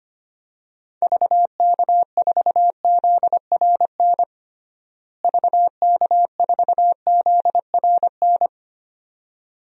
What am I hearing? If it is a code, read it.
Morse code, heard as a keyed tone, about 25 wpm, 700 Hz. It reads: VK4ZRN VK4ZRN
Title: DXCC-100 Call Signs - 6 Characters - Repeated 2x - 25wpm